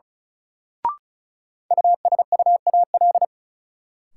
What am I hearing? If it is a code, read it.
Morse code, heard as a keyed tone, about 36 wpm, 700 Hz. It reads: USUAL
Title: Sentences from Top 400 Words - New Word Drill - 35wpm